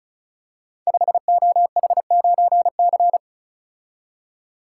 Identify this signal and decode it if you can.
Morse code, heard as a keyed tone, about 36 wpm, 700 Hz. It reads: 5OH9C
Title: Code Groups - 5 Letter:Numbers - 35wpm